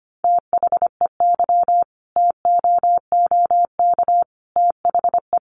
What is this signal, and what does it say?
Morse code, heard as a keyed tone, about 25 wpm, 700 Hz. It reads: THEY TOOK THE
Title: Sentences from Top 1200 Words 25wpm